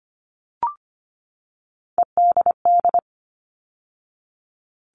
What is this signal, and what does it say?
Morse code, heard as a keyed tone, about 25 wpm, 700 Hz. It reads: EDD
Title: All Cut Numbers - 3-Digit 25wpm